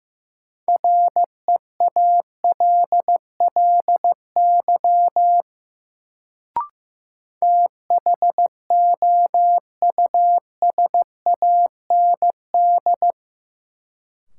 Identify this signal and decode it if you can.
Morse code, heard as a keyed tone, about 15 wpm, 700 Hz. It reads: REALLY THOUSAND
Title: Sentences from Top 600 Words - New Word Drill - 15wpm